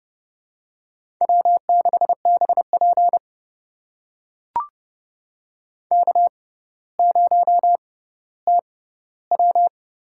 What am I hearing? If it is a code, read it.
Morse code, heard as a keyed tone, about 30 wpm, spaced as for 15 wpm, 700 Hz. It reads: W6BP K0TW
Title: USA Call Signs - 4 Characters - 6x Character Spacing - 30wpm